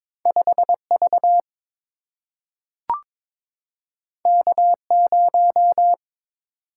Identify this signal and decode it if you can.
Morse code, heard as a keyed tone, about 22 wpm, 700 Hz. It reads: 5 V K 0
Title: Single-Letter Single-Number 22wpm